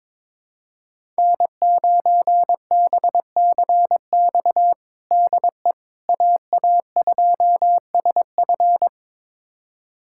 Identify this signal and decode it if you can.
Morse code, heard as a keyed tone, about 22 wpm, 700 Hz. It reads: N9BCX DE AA2SF
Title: USA Callsign DE Callsign - 5 Characters - 22wpm